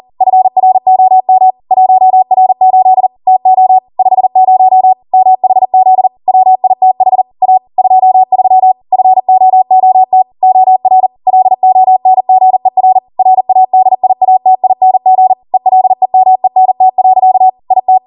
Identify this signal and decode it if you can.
Morse code, heard as a keyed tone, about 40 wpm, 750 Hz. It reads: FROM 1R8 TO 50 MHZ WITH A 23 FOOT OR LONGER RADIATING ELEMENT. IT